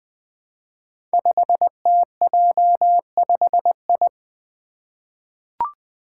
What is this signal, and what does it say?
Morse code, heard as a keyed tone, about 20 wpm, 700 Hz. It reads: 5TJ5I
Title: Five Challenging Letter-Number Combinations 20wpm